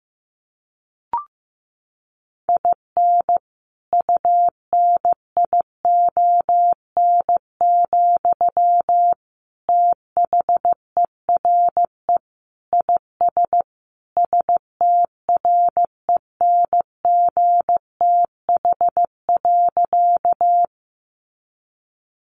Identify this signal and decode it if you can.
Morse code, heard as a keyed tone, about 15 wpm, 700 Hz. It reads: IN UNION, THERE IS STRENGTH.
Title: Sentences from Top 1400 Words 15wpm